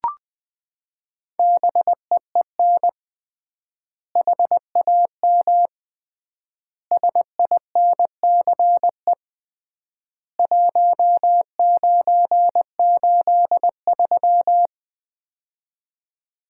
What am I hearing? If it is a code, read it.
Morse code, heard as a keyed tone, about 20 wpm, 700 Hz. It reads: BEEN HAM SINCE 1983
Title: Common QSO Phrases - 3x Word-Spacing - 20wpm